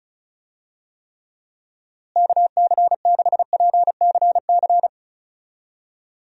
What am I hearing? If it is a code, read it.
Morse code, heard as a keyed tone, about 36 wpm, 700 Hz. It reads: KC6PCC
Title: USA Call Signs - 6 Characters - Repeated 3x - 35wpm